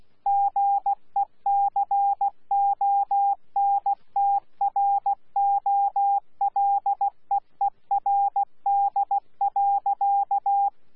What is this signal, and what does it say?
Morse code, heard as a keyed tone, about 16 wpm, 800 Hz. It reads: GECONTROLEERD.
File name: Z PI4VRZ / PI4VRZ 14-jun-2025